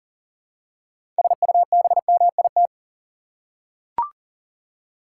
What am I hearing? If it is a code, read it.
Morse code, heard as a keyed tone, about 40 wpm, 700 Hz. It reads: SUBMIT
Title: Six Letter Words - 40wpm